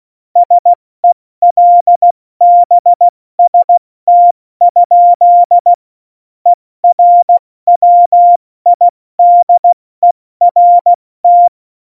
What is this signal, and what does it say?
Morse code, heard as a keyed tone, about 16 wpm, 700 Hz. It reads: SELBST? ERWIDERT
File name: Morsekurs / Witze / Witz 23. Juli 2019, 70 F80